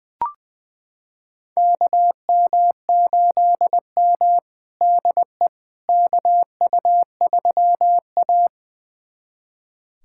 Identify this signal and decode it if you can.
Morse code, heard as a keyed tone, about 20 wpm, 700 Hz. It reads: KM8M DE KU3A
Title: USA Callsign DE Callsign - 4 Characters - 20wpm